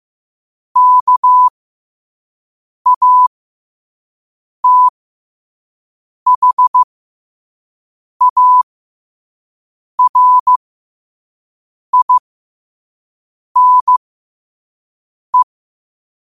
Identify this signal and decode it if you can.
Morse code, heard as a keyed tone, about 15 wpm, spaced as for 5 wpm, 1000 Hz. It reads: KATHARINE
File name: qotdfiles / 5wpm